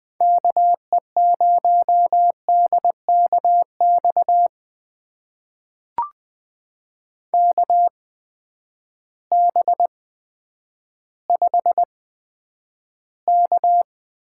Morse code, heard as a keyed tone, about 20 wpm, spaced as for 10 wpm, 700 Hz. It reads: KE0DKX KB5K
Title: USA Call Signs - 6 Characters - 8x Character Spacing - 20wpm